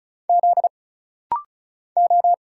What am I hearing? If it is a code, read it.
Morse code, heard as a keyed tone, about 36 wpm, 700 Hz. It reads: Z O
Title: Single Letter-Number Alt - Rapid-Fire 35wpm